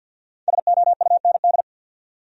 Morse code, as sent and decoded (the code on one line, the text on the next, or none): ... --- ..- -. -..
SOUND